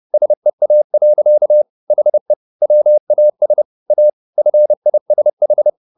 Morse code, heard as a keyed tone, about 30 wpm, 600 Hz. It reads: SEA. HE WAS A FISH